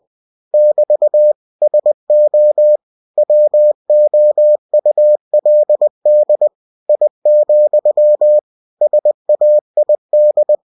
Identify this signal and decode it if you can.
Morse code, heard as a keyed tone, about 20 wpm, 600 Hz. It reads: <BT> SO WOULD I, SAID
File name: cw-books / winnie-the-pooh / winnie-20wpm-page-0108